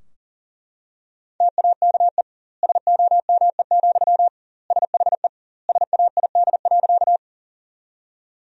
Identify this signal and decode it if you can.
Morse code, heard as a keyed tone, about 40 wpm, 700 Hz. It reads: TAKE SOME, SHE SAID.
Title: 2x Word-Spacing - Sentences from Top 200 Words - 40wpm